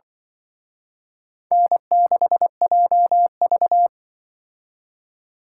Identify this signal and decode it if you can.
Morse code, heard as a keyed tone, about 25 wpm, 700 Hz. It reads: N6JV
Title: Speed Racing - USA 1x2 Call Signs 20wpm